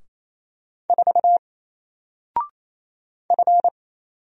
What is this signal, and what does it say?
Morse code, heard as a keyed tone, about 27 wpm, 700 Hz. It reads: 4 F
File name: Single-Letter-Number-28wpm